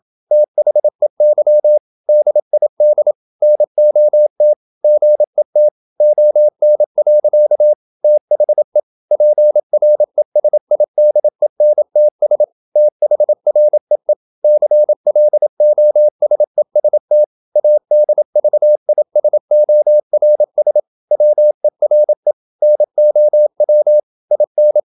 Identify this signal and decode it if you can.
Morse code, heard as a keyed tone, about 27 wpm, 600 Hz. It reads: THEY DID NOT GET ON. THE PRESIDENTS THREE CLOSEST ADVISORS WERE NOW IN
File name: Never 0027wpm